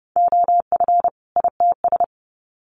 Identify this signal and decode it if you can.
Morse code, heard as a keyed tone, about 30 wpm, 700 Hz. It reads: OF ITS
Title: Sets of 2 Words 30wpm